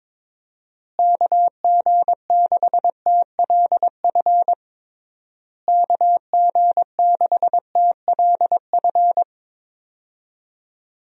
Morse code, heard as a keyed tone, about 22 wpm, 700 Hz. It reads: KG6TLF KG6TLF
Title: USA Call Signs - 6 Characters - Repeated 3x - 22wpm